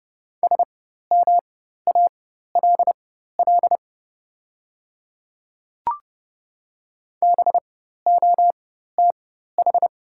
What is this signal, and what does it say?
Morse code, heard as a keyed tone, about 30 wpm, spaced as for 14 wpm, 700 Hz. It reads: SMALL BOTH
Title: 4x Character Spacing - Top 300 Words - 30wpm